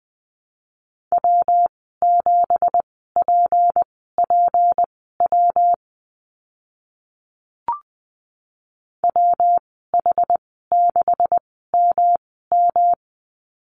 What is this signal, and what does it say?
Morse code, heard as a keyed tone, about 19 wpm, spaced as for 14 wpm, 700 Hz. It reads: W7PPW WH6MM
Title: 186 Lesson 30g - Letter b - Call Signs 20wpm